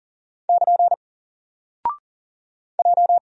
Morse code, heard as a keyed tone, about 39 wpm, 700 Hz. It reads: <KN> J
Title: Single Character (Letter:Number:Punctuation:Prosign) 40wpm